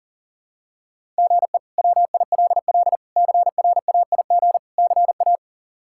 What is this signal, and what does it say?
Morse code, heard as a keyed tone, about 40 wpm, 700 Hz. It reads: GE WILL CRAIG CA
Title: K1USN SST QSO Practice - 40wpm